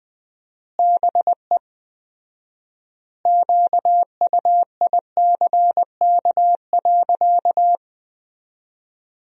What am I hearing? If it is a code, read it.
Morse code, heard as a keyed tone, about 20 wpm, 700 Hz. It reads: BE QUICK.
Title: 4x Word-Spacing - Sentences from Top 500 Words - 20wpm